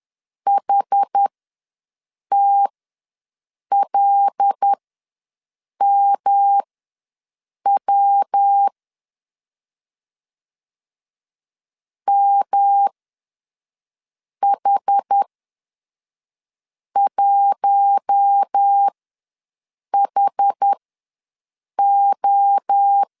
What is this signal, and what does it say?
Morse code, heard as a keyed tone, about 10 wpm, spaced as for 5 wpm, 800 Hz. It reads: HTLMW MH1HO
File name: LEZIONI / 5 LBVC / 5 L ws 10wpm 3636